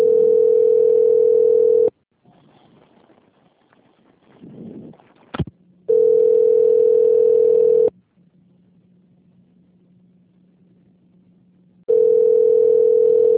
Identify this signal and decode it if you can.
Ringback tone (North America), 440 + 480 Hz.
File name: out- - - - - .24299